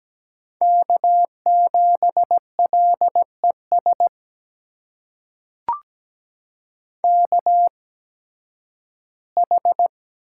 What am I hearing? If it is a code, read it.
Morse code, heard as a keyed tone, about 17 wpm, spaced as for 11 wpm, 700 Hz. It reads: K7LES KH
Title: USA Call Signs - 5 Characters - 8x Character Spacing - 17wpm